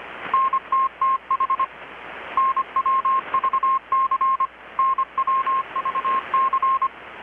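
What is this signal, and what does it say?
Morse code, heard as a keyed tone, about 25 wpm, 1070 Hz. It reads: NTTH NWVC NWVC
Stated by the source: Morse from KPH's Night of Nights XX maritime radiotelegraphy event, received in Chiapas, Mexico about 3,800 km away on a Kenwood R-600 with a grounded 10 m long-wire antenna.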